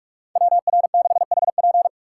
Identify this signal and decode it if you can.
Morse code, heard as a keyed tone, about 43 wpm, 690 Hz. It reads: WR6HP
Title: USA Call Signs - 5 Characters - 6x Character Spacing - 45wpm